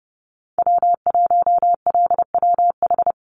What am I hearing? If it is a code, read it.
Morse code, heard as a keyed tone, about 30 wpm, 700 Hz. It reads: W1LWH
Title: Call Signs Repeated 3x - 30wpm